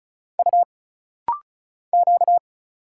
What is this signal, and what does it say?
Morse code, heard as a keyed tone, about 36 wpm, 700 Hz. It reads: U Q
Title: Single Letters - Rapid-Fire 35wpm